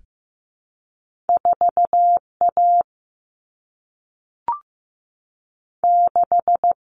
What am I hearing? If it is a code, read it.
Morse code, heard as a keyed tone, about 15 wpm, 700 Hz. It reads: 4 A 6
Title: Common Cut Numbers - 2-Digit 15wpm